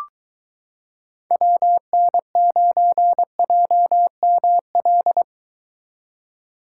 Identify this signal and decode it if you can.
Morse code, heard as a keyed tone, about 22 wpm, 700 Hz. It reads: WN9JML